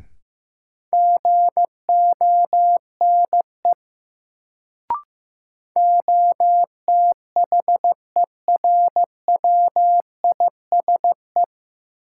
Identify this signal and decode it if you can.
Morse code, heard as a keyed tone, about 16 wpm, 700 Hz. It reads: GONE OTHERWISE
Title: Sentences from Top 1200 Words - New Word Drill Alt - 15wpm